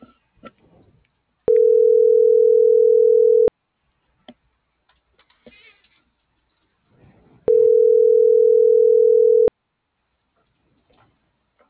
Ringback tone (North America), 440 + 480 Hz.
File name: CENTRAL MAERSK / out isabel.bravo